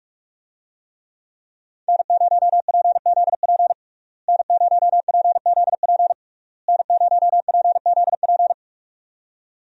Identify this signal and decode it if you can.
Morse code, heard as a keyed tone, about 46 wpm, 700 Hz. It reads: N0PZP N0PZP N0PZP
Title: USA Call Signs - 5 Characters - Repeated 3x - 45wpm